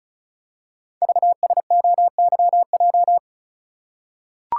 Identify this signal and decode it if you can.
Morse code, heard as a keyed tone, about 36 wpm, 700 Hz. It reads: VSOYJ
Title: Code Groups - 5 Letters - 35wpm